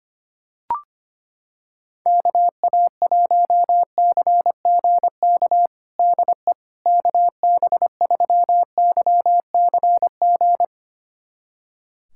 Morse code, heard as a keyed tone, about 25 wpm, 700 Hz. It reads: KA1CGK DE KB3YCG
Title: USA Callsign DE Callsign - 6 Characters - 25wpm